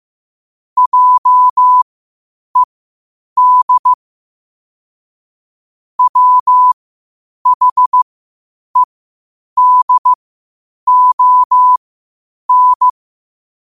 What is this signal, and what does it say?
Morse code, heard as a keyed tone, about 15 wpm, spaced as for 8 wpm, 1000 Hz. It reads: JED WHEDON
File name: qotdfiles / 8wpm